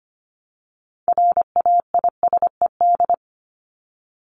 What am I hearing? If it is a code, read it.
Morse code, heard as a keyed tone, about 25 wpm, 700 Hz. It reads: RAISED
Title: ED Suffix - Top 100 Words with Base Words 25wpm